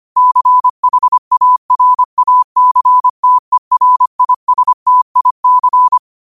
Morse code, heard as a keyed tone, about 25 wpm, 1000 Hz. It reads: CHARACTERISTIC